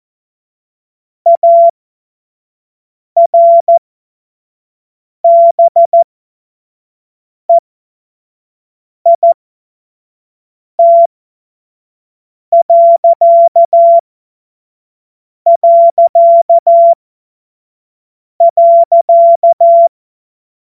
Morse code, heard as a keyed tone, about 14 wpm, spaced as for 5 wpm, 680 Hz. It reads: ARBEIT...